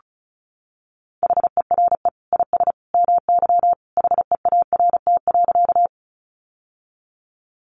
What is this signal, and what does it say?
Morse code, heard as a keyed tone, about 34 wpm, 700 Hz. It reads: HERE IS MY HEART.
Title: Sentences from Top 1200 Words 35wpm